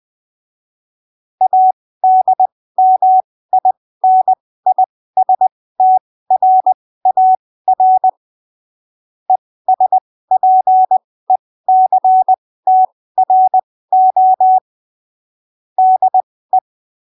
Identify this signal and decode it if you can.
Morse code, heard as a keyed tone, about 20 wpm, spaced as for 15 wpm, 750 Hz. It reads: ADMINISTRAR ESPECTRO DE
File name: arquivos / texto / texto 06 20ppm